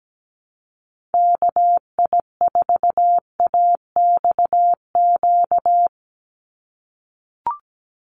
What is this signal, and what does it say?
Morse code, heard as a keyed tone, about 17 wpm, 700 Hz. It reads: KI4AXQ
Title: USA Call Signs 17wpm